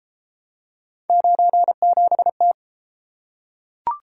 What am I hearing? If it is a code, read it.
Morse code, heard as a keyed tone, about 28 wpm, 700 Hz. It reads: 97T